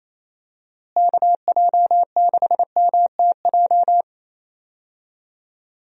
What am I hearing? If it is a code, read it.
Morse code, heard as a keyed tone, about 28 wpm, 700 Hz. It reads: KJ6MTJ
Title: USA Call Signs - 6 Characters - Repeated 3x - 28wpm